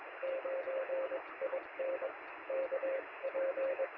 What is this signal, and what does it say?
Morse code, heard as a keyed tone, about 23 wpm, spaced as for 19 wpm, 560 Hz, with radio noise. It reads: 9 IN KP
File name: sm7lcb / 2139z oh3lwp 3cm